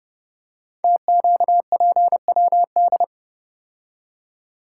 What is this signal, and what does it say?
Morse code, heard as a keyed tone, about 30 wpm, 700 Hz. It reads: TQPWD
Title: Code Groups - 5 Letters - 30wpm